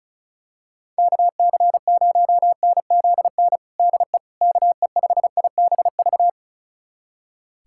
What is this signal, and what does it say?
Morse code, heard as a keyed tone, about 34 wpm, 690 Hz. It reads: KC0NZN DE KE5IBV
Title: USA Callsign DE Callsign 35wpm